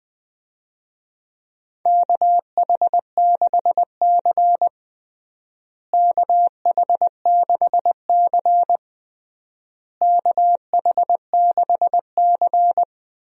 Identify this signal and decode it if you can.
Morse code, heard as a keyed tone, about 21 wpm, 700 Hz. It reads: KH6C KH6C KH6C
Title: DXCC-100 Call Signs - 4 Characters - Repeated 3x - 20wpm